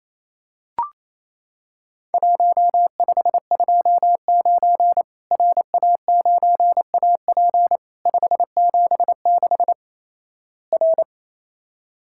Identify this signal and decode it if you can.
Morse code, heard as a keyed tone, about 28 wpm, 700 Hz. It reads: 1529 RA9AP 576 R
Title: Worked All Europe DX Contest - QTC Exchange - 28wpm